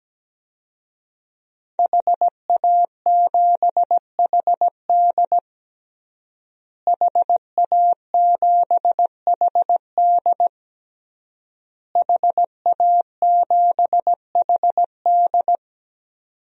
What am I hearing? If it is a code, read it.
Morse code, heard as a keyed tone, about 17 wpm, 700 Hz. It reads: HA7HD HA7HD HA7HD
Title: DXCC-100 Call Signs - 5 Characters - Repeated 3x - 17wpm